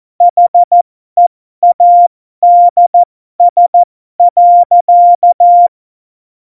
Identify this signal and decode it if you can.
Morse code, heard as a keyed tone, about 14 wpm, spaced as for 12 wpm, 700 Hz. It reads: HEADS.